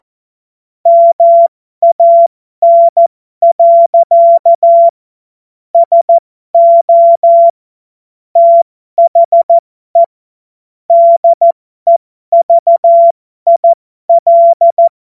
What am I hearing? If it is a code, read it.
Morse code, heard as a keyed tone, about 14 wpm, spaced as for 12 wpm, 680 Hz. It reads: MAN. SO THE DEVIL
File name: Morsekurs / Witze